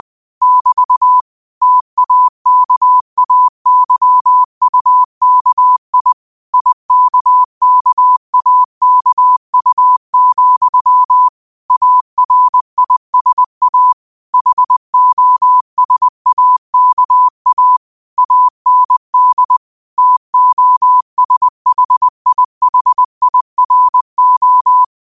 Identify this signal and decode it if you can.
Morse code, heard as a keyed tone, about 20 wpm, 1000 Hz. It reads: <BT> TAKAYUKI IKKAKU, ARISA HOSAKA AND TOSHIHIRO
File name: qotdfiles / 20wpm